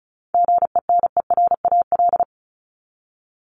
Morse code, heard as a keyed tone, about 34 wpm, 700 Hz. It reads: GENERAL
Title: Top 501-750 Words 35wpm